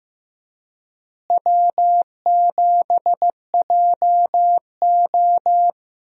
Morse code, heard as a keyed tone, about 15 wpm, 700 Hz. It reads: W7JO
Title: USA Call Signs - 4 Characters - Repeated 2x - 15wpm